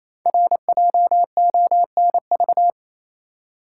Morse code, heard as a keyed tone, about 28 wpm, 700 Hz. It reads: RJONV
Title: Code Groups - 5 Letters - 28wpm